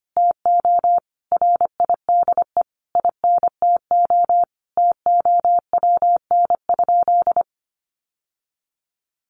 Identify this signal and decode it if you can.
Morse code, heard as a keyed tone, about 25 wpm, 700 Hz. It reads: TO RIDE INTO TOWN?
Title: Sentences from Top 1400 Words 25wpm